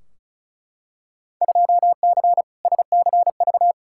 Morse code, heard as a keyed tone, about 36 wpm, 690 Hz. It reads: 2C SCV
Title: Field Day Exchanges Encore - 35wpm